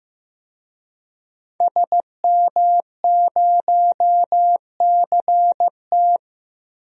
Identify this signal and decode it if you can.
Morse code, heard as a keyed tone, about 15 wpm, 700 Hz. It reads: SM0CT
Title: DXCC-100 Call Signs - 5 Characters - 15wpm